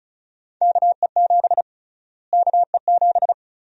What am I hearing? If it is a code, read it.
Morse code, heard as a keyed tone, about 34 wpm, 700 Hz. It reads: KE7 KE7
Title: USA Call Sign Prefixes - Repeated 2x - 35wpm